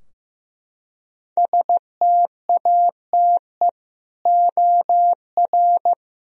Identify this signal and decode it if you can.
Morse code, heard as a keyed tone, about 15 wpm, 700 Hz. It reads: STATE OR
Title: Sentences from Top 1000 Words - Review Part 3 - 15wpm